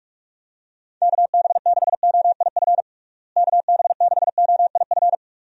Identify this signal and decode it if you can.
Morse code, heard as a keyed tone, about 46 wpm, 700 Hz. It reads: KB6OIF KB6OIF
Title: USA Call Signs - 6 Characters - Repeated 2x - 45wpm